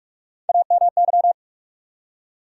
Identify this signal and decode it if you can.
Morse code, heard as a keyed tone, about 47 wpm, 700 Hz. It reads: AMY
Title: USA Common Names - Max 4 Letters - 45wpm